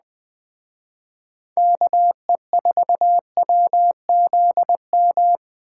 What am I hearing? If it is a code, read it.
Morse code, heard as a keyed tone, about 20 wpm, 700 Hz. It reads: KE4WZM
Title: Speed Racing - USA Call Signs 17wpm